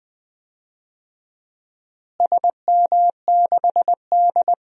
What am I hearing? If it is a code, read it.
Morse code, heard as a keyed tone, about 20 wpm, 700 Hz. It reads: SM6D
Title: DXCC-100 Call Signs - 4 Characters - Repeated 2x - 20wpm